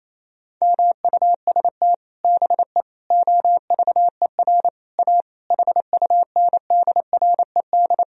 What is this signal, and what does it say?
Morse code, heard as a keyed tone, about 27 wpm, 700 Hz. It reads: MUST BE OVER A HUNDRED
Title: Sentences from Top 1000 Words - Review Part 2 - 28wpm